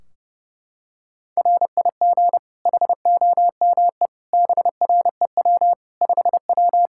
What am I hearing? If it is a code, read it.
Morse code, heard as a keyed tone, about 30 wpm, 700 Hz. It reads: RIG HOME BREW 5W
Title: K1USN SST QSO Practice 30wpm